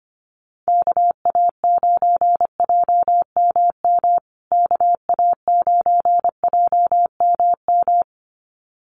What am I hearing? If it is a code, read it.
Morse code, heard as a keyed tone, about 25 wpm, 700 Hz. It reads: KA9JMM KA9JMM
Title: USA Call Signs Repeated 2x 25wpm